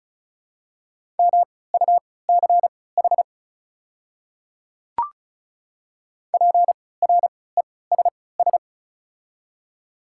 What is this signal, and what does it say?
Morse code, heard as a keyed tone, about 36 wpm, spaced as for 20 wpm, 700 Hz. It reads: MUCH PRESS
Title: 3x Character Spacing - Top 300 Words - 35wpm